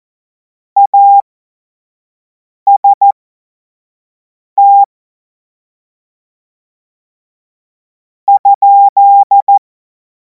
Morse code, heard as a keyed tone, about 14 wpm, spaced as for 4 wpm, 800 Hz. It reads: AST?
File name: Morsekurs / Witze / Witz 14. Juli 2026 25 F70